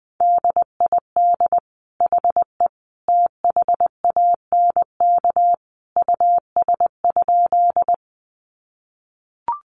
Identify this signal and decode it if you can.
Morse code, heard as a keyed tone, about 20 wpm, 700 Hz. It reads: DID HE THANK US?
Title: Sentences from Top 600 Words 20wpm